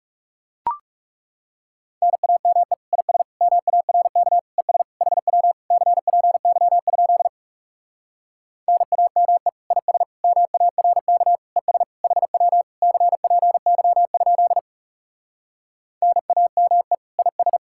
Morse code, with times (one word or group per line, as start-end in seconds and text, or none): 2.0-2.7 NAME
2.9-3.2 IS
3.4-4.4 MARK
4.6-4.8 ES
5.0-5.5 HW
5.7-7.3 CPY?
8.7-9.5 NAME
9.7-10.0 IS
10.2-11.4 MARK
11.6-11.8 ES
12.0-12.6 HW
12.8-14.6 CPY?
16.0-16.9 NAME
17.2-17.6 IS